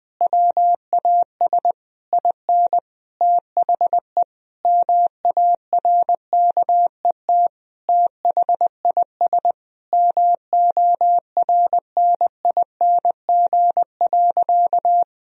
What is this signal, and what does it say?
Morse code, heard as a keyed tone, about 20 wpm, 700 Hz. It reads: WAS IN THE MARKET THIS MORNING.
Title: Sentences from Top 1000 Words - Review Part 3 - 20wpm